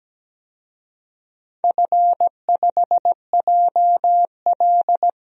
Morse code, heard as a keyed tone, about 17 wpm, 700 Hz. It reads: F5JL
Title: DXCC-100 Call Signs - 4 Characters - 17wpm